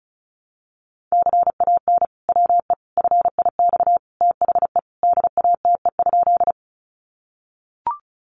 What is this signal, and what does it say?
Morse code, heard as a keyed tone, about 34 wpm, 700 Hz. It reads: CAN WE FIX THE DATE?
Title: Sentences from Top 1400 Words 35wpm